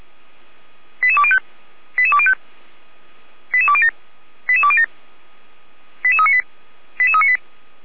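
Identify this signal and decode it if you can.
Five-tone selective call (ZVEI), heard as quick runs of five tones, tones of about 70 ms. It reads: 88205 88205 88207 88207 88398 88398
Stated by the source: ZVEI five-tone selective call received with an SDR.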